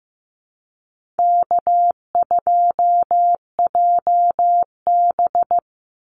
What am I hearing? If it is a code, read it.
Morse code, heard as a keyed tone, about 15 wpm, 700 Hz. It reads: K2JB
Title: NA SOTA Activator Callsigns 15wpm